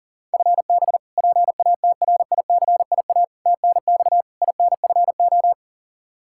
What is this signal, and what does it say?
Morse code, heard as a keyed tone, about 40 wpm, 700 Hz. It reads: FB PATRICIA TNX INFO
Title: Practice QSO - Exchange 3 - More Info - 40wpm